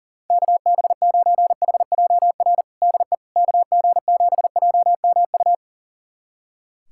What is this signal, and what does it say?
Morse code, heard as a keyed tone, about 40 wpm, 700 Hz. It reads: KB9HJR DE KG7JMU
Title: USA Callsign DE Callsign - 6 Characters - 40wpm